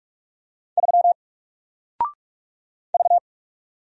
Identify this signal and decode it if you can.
Morse code, heard as a keyed tone, about 44 wpm, 700 Hz. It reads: 3 V
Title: Single Letter-Number 45wpm